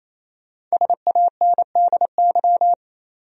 Morse code, heard as a keyed tone, about 28 wpm, 700 Hz. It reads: SANDY
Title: USA Common Names - Max 5 Letters - 28wpm